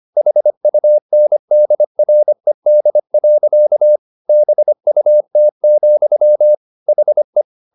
Morse code, heard as a keyed tone, about 25 wpm, 600 Hz. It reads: HUNDRED. BUT, HE
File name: the old man and the sea part 1 25wpm